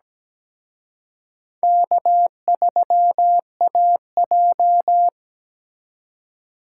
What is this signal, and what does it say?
Morse code, heard as a keyed tone, about 20 wpm, 700 Hz. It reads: K3AJ
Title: Speed Racing - USA 1x2 Call Signs 17wpm